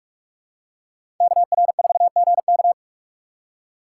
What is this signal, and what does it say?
Morse code, heard as a keyed tone, about 46 wpm, 700 Hz. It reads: KR4GK
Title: USA Call Signs - 2x Character Spacing - 45wpm